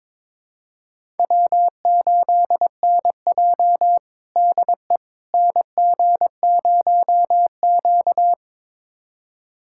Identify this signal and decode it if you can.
Morse code, heard as a keyed tone, about 22 wpm, 700 Hz. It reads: W8NJ DE NG0Q
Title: USA Callsign DE Callsign - 4 Characters - 22wpm